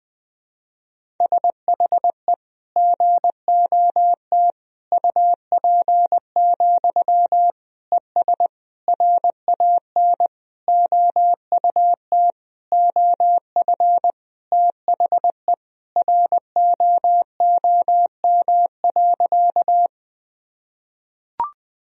Morse code, heard as a keyed tone, about 20 wpm, 700 Hz. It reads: SHE GOT UP, ES RAN OUT OF THE ROOM.
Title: Sentences from Top 500 Words and Abbreviations 20wpm